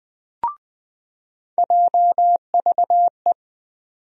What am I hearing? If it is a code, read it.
Morse code, heard as a keyed tone, about 20 wpm, 700 Hz. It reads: JVE
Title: USA Call Sign Postfixes - 3 Letters 20wpm